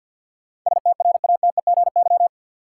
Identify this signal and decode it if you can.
Morse code, heard as a keyed tone, about 46 wpm, 700 Hz. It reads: STRATEGY
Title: Sentences from Top 1000 Words - Review Part 5 - New Word Drill - 50wpm